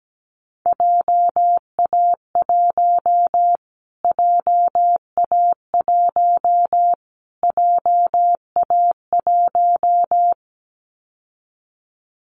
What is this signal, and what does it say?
Morse code, heard as a keyed tone, about 17 wpm, 700 Hz. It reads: JA1 JA1 JA1
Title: Callsign Prefixes for Top 25 DXCC - Repeated 3x - 17wpm